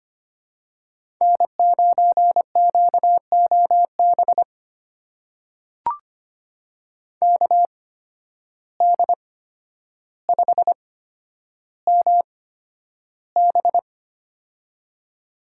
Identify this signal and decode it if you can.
Morse code, heard as a keyed tone, about 25 wpm, spaced as for 11 wpm, 700 Hz. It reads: N9QOB KD5MB
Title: USA Call Signs - 5 Characters - 8x Character Spacing - 25wpm